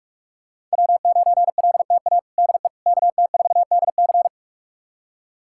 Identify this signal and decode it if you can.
Morse code, heard as a keyed tone, about 46 wpm, 700 Hz. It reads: W9LTA DE KT4DC
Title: USA Callsign DE Callsign - 5 Characters - 45wpm